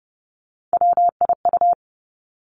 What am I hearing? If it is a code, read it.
Morse code, heard as a keyed tone, about 32 wpm, 700 Hz. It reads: WIU
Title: Three Challenging Letter-Number Combinations 30wpm